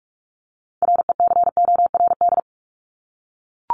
Morse code, heard as a keyed tone, about 44 wpm, 700 Hz. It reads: RECORD
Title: Top 500 Words Repeated 45wpm